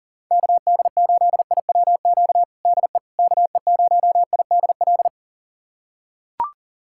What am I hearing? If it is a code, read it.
Morse code, heard as a keyed tone, about 40 wpm, 700 Hz. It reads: KD8IWQ DE KE0IDL
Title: USA Callsign DE Callsign - 6 Characters - 40wpm